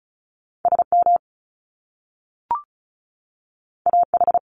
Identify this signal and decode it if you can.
Morse code, heard as a keyed tone, about 36 wpm, 700 Hz. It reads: SM AH
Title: Top Two Letter Combinations in Words 35wpm